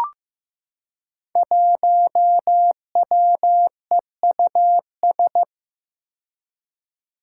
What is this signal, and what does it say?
Morse code, heard as a keyed tone, about 15 wpm, 700 Hz. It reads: 1WEUS
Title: Five Challenging Letter-Number Combinations 15wpm